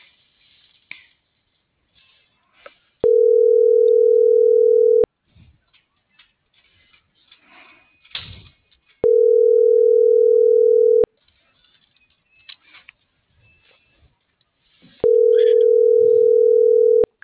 Ringback tone (North America), 440 + 480 Hz.